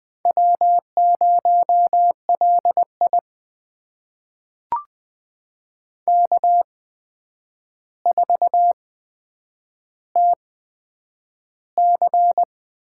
Morse code, heard as a keyed tone, about 20 wpm, spaced as for 9 wpm, 700 Hz. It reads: W0LI K4TC
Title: USA Call Signs - 4 Characters - 8x Character Spacing - 20wpm